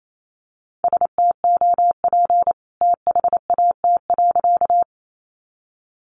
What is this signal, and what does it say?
Morse code, heard as a keyed tone, about 28 wpm, 700 Hz. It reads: STOP THAT.
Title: A Sound Of Thunder 28wpm